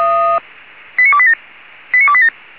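Five-tone selective call (ZVEI), heard as quick runs of five tones, tones of about 80 ms. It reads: 89287 89287
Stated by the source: ZVEI five-tone call 89299 received on SDR.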